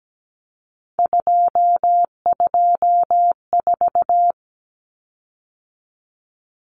Morse code, heard as a keyed tone, about 17 wpm, 700 Hz. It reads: 224
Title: Common Cut Numbers - 3-Digit 17wpm